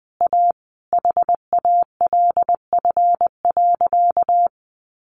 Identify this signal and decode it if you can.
Morse code, heard as a keyed tone, about 20 wpm, 700 Hz. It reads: A HALF.
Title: Sentences from Top 400 Words 20wpm